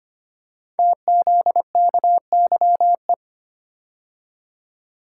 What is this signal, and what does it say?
Morse code, heard as a keyed tone, about 25 wpm, 700 Hz. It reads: TZKYE
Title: Code Groups - 5 Letters - 25wpm